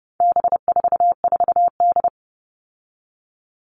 Morse code, heard as a keyed tone, about 30 wpm, 700 Hz. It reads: B44D
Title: All Cut Numbers - 4-Digit 30wpm